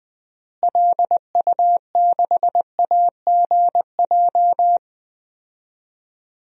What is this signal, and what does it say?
Morse code, heard as a keyed tone, about 20 wpm, 700 Hz. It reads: LU6AGJ
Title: DXCC-100 Call Signs - 6 Characters - Repeated 3x - 20wpm